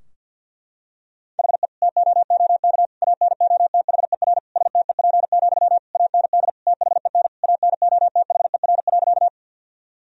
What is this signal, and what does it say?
Morse code, heard as a keyed tone, about 46 wpm, 700 Hz. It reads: HE TOOK ANOTHER STEP, AND THEN ANOTHER.
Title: Sentences from Top 1000 Words - Review Part 2 - 50wpm